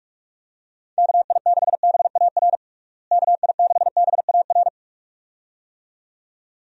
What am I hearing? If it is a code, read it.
Morse code, heard as a keyed tone, about 46 wpm, 700 Hz. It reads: KI6BAR KI6BAR
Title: USA Call Signs - 6 Characters - Repeated 2x - 45wpm